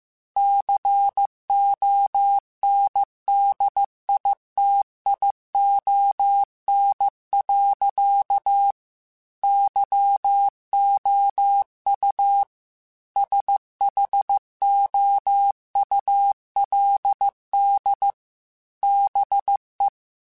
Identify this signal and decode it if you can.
Morse code, heard as a keyed tone, about 15 wpm, 800 Hz. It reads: CONDITION. YOU SHOULD BE
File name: Technical Subjects / Technical Subjects-mp3s / OR Gate @13